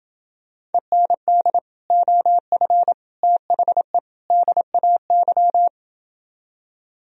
Morse code, heard as a keyed tone, about 29 wpm, 700 Hz. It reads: END OF THE DAY